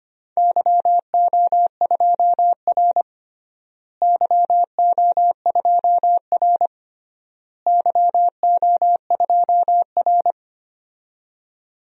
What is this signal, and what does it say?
Morse code, heard as a keyed tone, about 25 wpm, 700 Hz. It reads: YO2R YO2R YO2R
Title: DXCC-100 Call Signs - 4 Characters - Repeated 3x - 25wpm